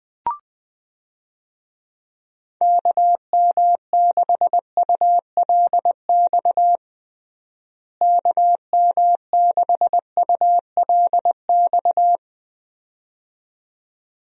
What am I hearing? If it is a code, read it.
Morse code, heard as a keyed tone, about 20 wpm, 700 Hz. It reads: KM6ULX KM6ULX
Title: USA Call Signs - 6 Characters - Repeated 2x - 20wpm